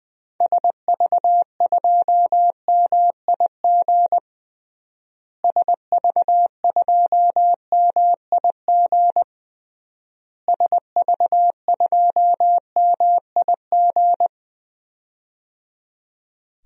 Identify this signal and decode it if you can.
Morse code, heard as a keyed tone, about 19 wpm, 700 Hz. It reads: SV2MIG SV2MIG SV2MIG
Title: DXCC-100 Call Signs - 6 Characters - Repeated 3x - 20wpm